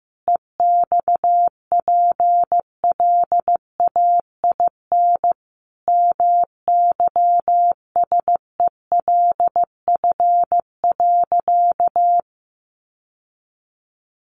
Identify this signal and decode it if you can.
Morse code, heard as a keyed tone, about 15 wpm, 700 Hz. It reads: EXPLAIN MYSELF.
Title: Sentences from Top 1300 Words 15wpm